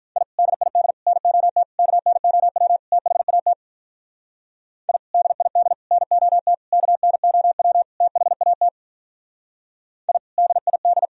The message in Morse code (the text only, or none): I DID NOT KNOW THAT I DID NOT KNOW THAT I DID